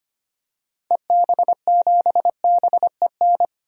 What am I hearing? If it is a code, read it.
Morse code, heard as a keyed tone, about 25 wpm, 700 Hz. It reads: EB7BEN
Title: DXCC-100 Call Signs - 6 Characters - 25wpm